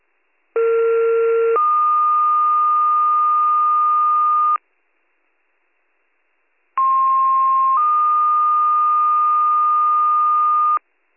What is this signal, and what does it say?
Two-tone page, heard as a one-second tone, then a three-second tone, about 460 then 1220 Hz, about 1060 then 1220 Hz. It is heard from 0.6 to 10.8 s.